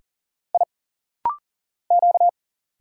Morse code, heard as a keyed tone, about 40 wpm, 700 Hz. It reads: I Q
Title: Single Letter-Number - Rapid-Fire 40wpm